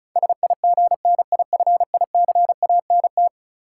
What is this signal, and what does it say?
Morse code, heard as a keyed tone, about 36 wpm, 700 Hz. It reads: SIGNIFICANT
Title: Sentences from Top 1600 Words - New Word Drill Alt - 35wpm